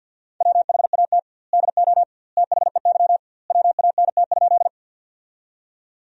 Morse code, heard as a keyed tone, about 50 wpm, 700 Hz. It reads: WHAT DO THEY WANT?
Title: 2x Word-Spacing - Sentences from Top 200 Words - 50wpm